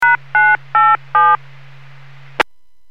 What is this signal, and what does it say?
DTMF, heard as telephone touch tones, tones of about 230 ms. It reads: #987